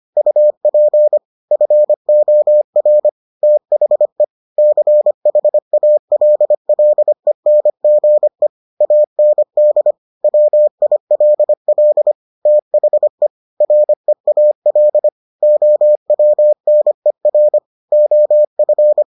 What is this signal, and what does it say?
Morse code, heard as a keyed tone, about 25 wpm, 600 Hz. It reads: UP FOR THE CHALLENGE AND WILL THE REAL OWNER OF